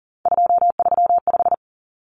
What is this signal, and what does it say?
Morse code, heard as a keyed tone, about 40 wpm, 700 Hz. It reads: 235
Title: Common Cut Numbers - 3-Digit 40wpm